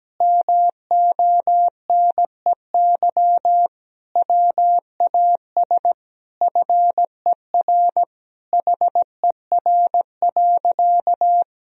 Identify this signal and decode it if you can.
Morse code, heard as a keyed tone, about 17 wpm, 700 Hz. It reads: MONEY WAS FER HER.
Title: Sentences from Top 400 Words and Abbreviations 17wpm